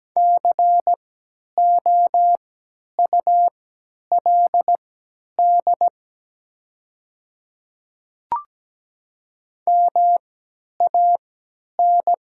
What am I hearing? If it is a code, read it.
Morse code, heard as a keyed tone, about 17 wpm, spaced as for 10 wpm, 700 Hz. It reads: COULD MAN
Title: 3x Character Spacing - Top 200 Words - 17wpm